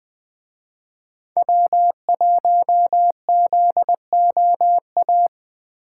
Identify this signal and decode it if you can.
Morse code, heard as a keyed tone, about 23 wpm, 700 Hz. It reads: W1ZOA
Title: Speed Racing - USA 1x3 Call Signs 20wpm